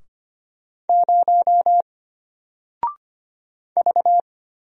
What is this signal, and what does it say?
Morse code, heard as a keyed tone, about 25 wpm, 700 Hz. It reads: 0 V
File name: Single-Letter-Number-25wpm